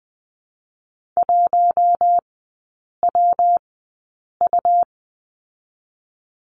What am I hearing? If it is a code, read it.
Morse code, heard as a keyed tone, about 20 wpm, spaced as for 8 wpm, 700 Hz. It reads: 1WU
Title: 198 Lesson 32e - Letter k - Three Characters 20wpm